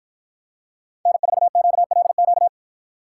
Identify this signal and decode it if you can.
Morse code, heard as a keyed tone, about 49 wpm, 700 Hz. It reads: N4QLQ